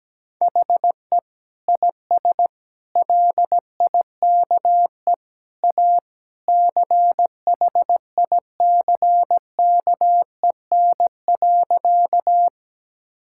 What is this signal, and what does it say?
Morse code, heard as a keyed tone, about 17 wpm, 700 Hz. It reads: HE IS LIKE A CHICKEN.
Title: Sentences from Top 1000 Words - Review Part 5 - 17wpm